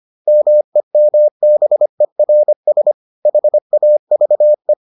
Morse code, heard as a keyed tone, about 25 wpm, 600 Hz. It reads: MEMBERS HAVE